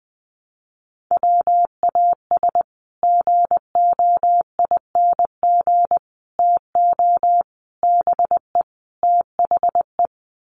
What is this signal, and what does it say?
Morse code, heard as a keyed tone, about 20 wpm, 700 Hz. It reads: WAS GOING TO BE THE
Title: Sets of 5 Words Encore2 20wpm